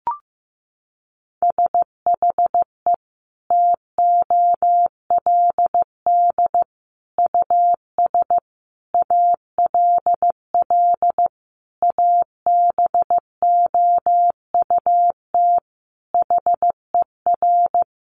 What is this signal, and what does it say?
Morse code, heard as a keyed tone, about 15 wpm, 700 Hz. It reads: SHE TOLD US ALL ABOUT HER
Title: Sentences from Top 1200 Words 15wpm